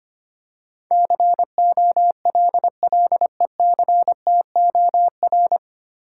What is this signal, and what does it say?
Morse code, heard as a keyed tone, about 25 wpm, 690 Hz. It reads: COLLECTOR
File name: practice / 25wpm / 9 Letter Words - First 3 Letters Given - 25wpm